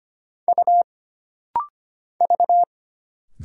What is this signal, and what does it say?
Morse code, heard as a keyed tone, about 25 wpm, 700 Hz. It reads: U V
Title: Single Letters - Rapid-Fire 25wpm